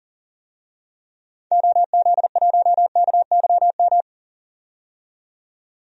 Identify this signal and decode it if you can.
Morse code, heard as a keyed tone, about 40 wpm, 700 Hz. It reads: OZ1KYM
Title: DXCC-100 Call Signs - 6 Characters - 40wpm